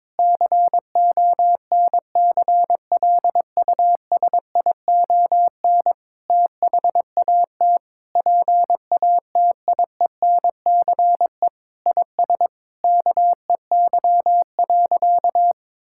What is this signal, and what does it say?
Morse code, heard as a keyed tone, about 22 wpm, 700 Hz. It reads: CONCLUSION THAT PATIENCE IS KEY.
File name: Sentences with 7-Word Phrases - 22wpm